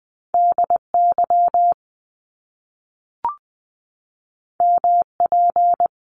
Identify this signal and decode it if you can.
Morse code, heard as a keyed tone, about 20 wpm, 700 Hz. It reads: DY MP
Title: Top Two Letter Combinations in Words 20wpm